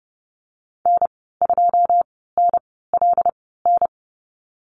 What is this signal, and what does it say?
Morse code, heard as a keyed tone, about 30 wpm, spaced as for 17 wpm, 700 Hz. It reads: N2NLN
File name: Lesson 15g - Number 5 - Call Signs 30wpm